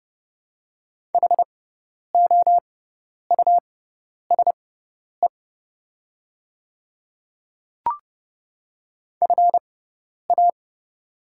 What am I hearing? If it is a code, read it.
Morse code, heard as a keyed tone, about 30 wpm, spaced as for 10 wpm, 700 Hz. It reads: HOUSE FA
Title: 6x Character Spacing - Top 300 Words - 30wpm